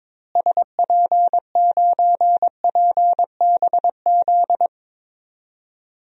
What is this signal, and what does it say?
Morse code, heard as a keyed tone, about 22 wpm, 700 Hz. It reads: SP9PBZ
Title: DXCC-100 Call Signs - 6 Characters - Repeated 2x - 22wpm